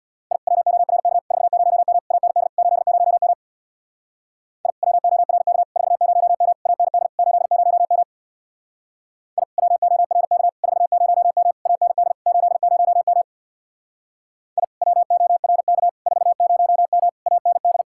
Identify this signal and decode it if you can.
Morse code, heard as a keyed tone, about 64 wpm, 700 Hz. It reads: I WORK 40M AND 80M I WORK 40M AND 80M I WORK 40M AND 80M I WORK 40M AND